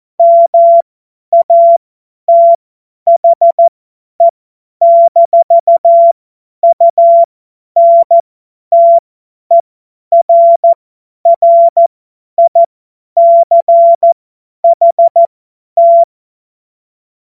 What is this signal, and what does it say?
Morse code, heard as a keyed tone, about 14 wpm, spaced as for 10 wpm, 680 Hz. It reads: MATHE-UNTERRICHT
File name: Morsekurs / Witze / Witz 14. März 2023 50 F70